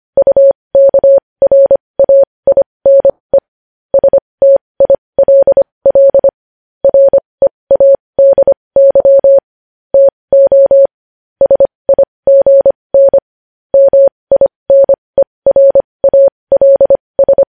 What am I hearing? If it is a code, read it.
Morse code, heard as a keyed tone, about 25 wpm, spaced as for 20 wpm, 550 Hz. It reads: UKRAINE STILL READY TO SIGN MINERALS
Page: news-headlines-in-morse-code-at-20-wpm